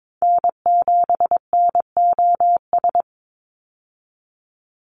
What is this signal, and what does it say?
Morse code, heard as a keyed tone, about 22 wpm, 700 Hz. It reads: N7NOS
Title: USA Call Signs 22wpm